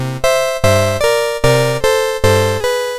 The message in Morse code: none